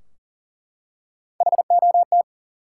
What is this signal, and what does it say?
Morse code, heard as a keyed tone, about 38 wpm, 700 Hz. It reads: HOT
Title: Three Letter Words - 40wpm